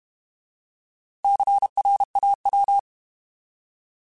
Morse code, heard as a keyed tone, about 32 wpm, 800 Hz. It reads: CRAW
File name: Alphabetical / Alphabetical mp3s / Cs @30